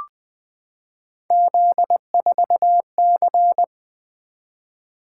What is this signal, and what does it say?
Morse code, heard as a keyed tone, about 20 wpm, 700 Hz. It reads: Z4C
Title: Three Letter-Number Combinations - 20wpm